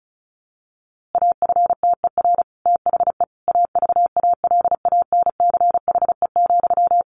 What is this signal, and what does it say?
Morse code, heard as a keyed tone, about 34 wpm, 700 Hz. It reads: AFTER THE AVALANCHE,
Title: A Sound Of Thunder 35wpm